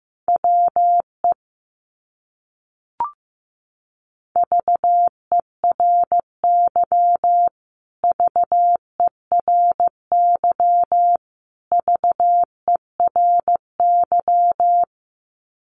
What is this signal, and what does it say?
Morse code, heard as a keyed tone, about 15 wpm, 700 Hz. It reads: WE VERY VERY VERY
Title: Top 200 Words Repeated 15wpm